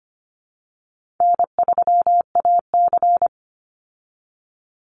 Morse code, heard as a keyed tone, about 25 wpm, 700 Hz. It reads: N3AC
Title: Easy Call Signs Repeated 2x - 25wpm